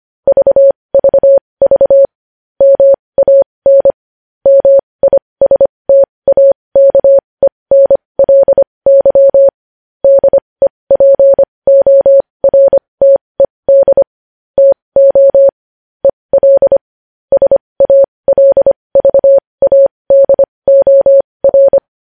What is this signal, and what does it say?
Morse code, heard as a keyed tone, about 25 wpm, spaced as for 20 wpm, 550 Hz. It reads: VVV MAN MISTAKENLY DEPORTED TO EL SALVADOR
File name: podcast / News headlines-1x-25x20